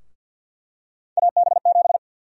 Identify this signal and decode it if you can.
Morse code, heard as a keyed tone, about 51 wpm, 700 Hz. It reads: AB7
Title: USA Call Sign Prefixes - Repeated 2x - 50wpm